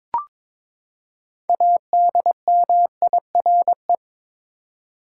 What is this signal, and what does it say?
Morse code, heard as a keyed tone, about 21 wpm, 700 Hz. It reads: ADMIRE
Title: Six Letter Words - 22wpm